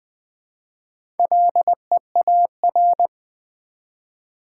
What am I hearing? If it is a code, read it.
Morse code, heard as a keyed tone, about 20 wpm, 700 Hz. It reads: LEAR
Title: Build A Word - 1 Letter At A Time - 5-Letter Words - 20wpm